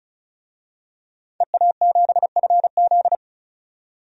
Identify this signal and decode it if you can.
Morse code, heard as a keyed tone, about 34 wpm, 700 Hz. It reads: EA7FZ
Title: DXCC-100 Call Signs - 5 Characters - 35wpm